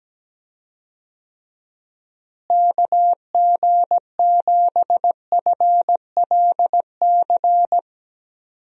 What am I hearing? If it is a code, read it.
Morse code, heard as a keyed tone, about 17 wpm, 700 Hz. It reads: KG7FLC
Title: USA Call Signs - 6 Characters - Repeated 2x - 17wpm